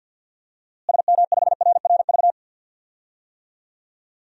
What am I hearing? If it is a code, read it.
Morse code, heard as a keyed tone, about 48 wpm, 700 Hz. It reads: SM5RRV
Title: DXCC-100 Call Signs - 6 Characters - Repeated 3x - 50wpm